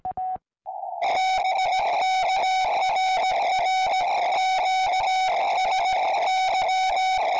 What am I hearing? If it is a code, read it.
Morse code, heard as a keyed tone, about 17 wpm, 1500 Hz, with noise. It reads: TIKRR QSY
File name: tropo